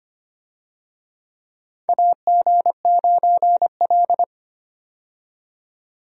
Morse code, heard as a keyed tone, about 25 wpm, 700 Hz. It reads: AG9L
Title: USA Call Signs - 4 Characters - 25wpm